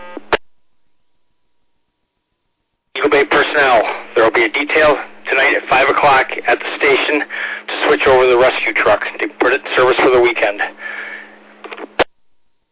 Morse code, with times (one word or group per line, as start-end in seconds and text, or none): none